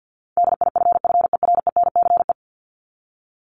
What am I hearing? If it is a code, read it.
Morse code, heard as a keyed tone, about 50 wpm, 700 Hz. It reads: DIFFERENCE
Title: Top 751-1000 Words 50wpm